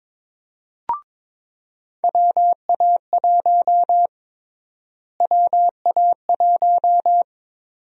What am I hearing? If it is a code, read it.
Morse code, heard as a keyed tone, about 21 wpm, 700 Hz. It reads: WA1 WA1
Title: USA Call Sign Prefixes - Repeated 2x - 22wpm